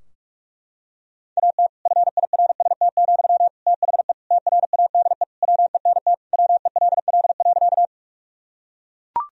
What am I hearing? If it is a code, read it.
Morse code, heard as a keyed tone, about 42 wpm, 700 Hz. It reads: AT FIRST, THE TRADE WENT WELL.
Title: Sentences from Top 1000 Words - Review Part 4 - 45wpm